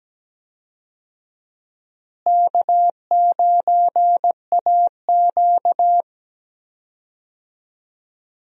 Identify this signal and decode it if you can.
Morse code, heard as a keyed tone, about 17 wpm, 700 Hz. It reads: K9AQ
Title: USA Call Signs - 4 Characters - Repeated 2x - 17wpm